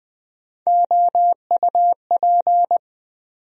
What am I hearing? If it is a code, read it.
Morse code, heard as a keyed tone, about 20 wpm, 700 Hz. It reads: OUP
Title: USA Call Sign Postfixes - 3 Letters 20wpm